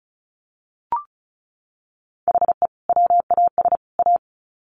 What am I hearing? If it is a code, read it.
Morse code, heard as a keyed tone, about 34 wpm, 700 Hz. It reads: HE WAS A
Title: Sets of 3 Words 35wpm